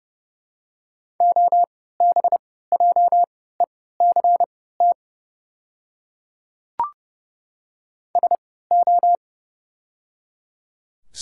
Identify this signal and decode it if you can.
Morse code, heard as a keyed tone, about 29 wpm, spaced as for 17 wpm, 700 Hz. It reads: OBJECT SO
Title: 3x Character Spacing - Top 500 Words - 30wpm